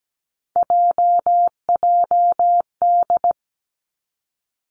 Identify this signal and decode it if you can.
Morse code, heard as a keyed tone, about 17 wpm, 700 Hz. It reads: JJD
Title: Three Challenging Letter-Number Combinations 17wpm